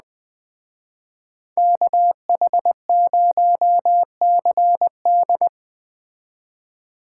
Morse code, heard as a keyed tone, about 20 wpm, 700 Hz. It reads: KH0CD
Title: Speed Racing - USA 2x2 Call Signs 17wpm